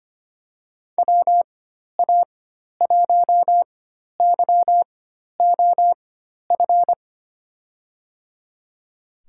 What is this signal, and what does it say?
Morse code, heard as a keyed tone, about 25 wpm, spaced as for 12 wpm, 700 Hz. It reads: WA1YOF
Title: USA Call Signs - 6 Characters - 4x Character Spacing - 25wpm